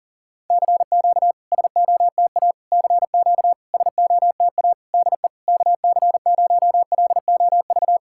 Morse code, heard as a keyed tone, about 40 wpm, 700 Hz. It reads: CQ SOTA CQ SOTA DE KC0LOV
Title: SOTA QSO Practice 40wpm